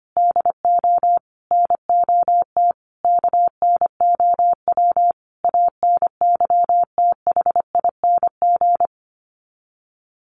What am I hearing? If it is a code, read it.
Morse code, heard as a keyed tone, about 25 wpm, 700 Hz. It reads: DO NOT KNOW ANYTHING
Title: Sets of 5 Words Encore 25wpm